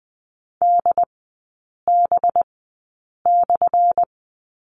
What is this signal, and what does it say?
Morse code, heard as a keyed tone, about 20 wpm, spaced as for 8 wpm, 700 Hz. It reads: DB/
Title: 184 Lesson 30e - Letter b - Three Characters 20wpm